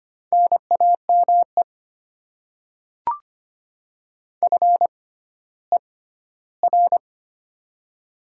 Morse code, heard as a keyed tone, about 25 wpm, spaced as for 14 wpm, 700 Hz. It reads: NAME FER
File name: Top 20 QSO Elements - 6x Character Spacing - 25wpm